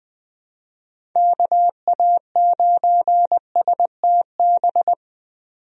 Morse code, heard as a keyed tone, about 21 wpm, 700 Hz. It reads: KA9STB
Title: USA Call Signs - 6 Characters - 20wpm